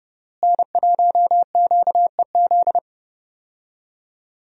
Morse code, heard as a keyed tone, about 30 wpm, 700 Hz. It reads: N1QEZ
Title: USA Call Signs - 5 Characters - Repeated 3x - 30wpm